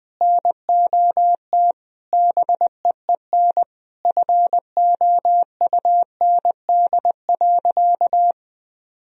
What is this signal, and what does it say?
Morse code, heard as a keyed tone, about 21 wpm, 700 Hz. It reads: NOT BEEN FOUND.
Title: Sentences from Top 1000 Words - Review Part 4 - 20wpm